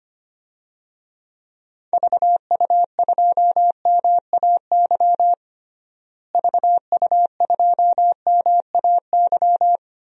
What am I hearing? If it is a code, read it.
Morse code, heard as a keyed tone, about 25 wpm, 700 Hz. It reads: VU2MAY VU2MAY
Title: DXCC-100 Call Signs - 6 Characters - Repeated 2x - 25wpm